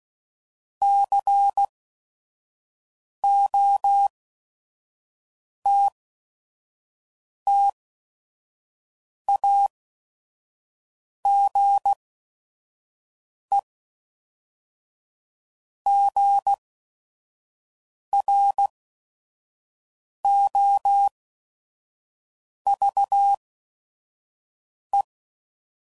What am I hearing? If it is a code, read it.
Morse code, heard as a keyed tone, about 16 wpm, spaced as for 5 wpm, 800 Hz. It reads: COTTAGEGROVE
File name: CW Code Groups @5WPM / fromSM's / Cities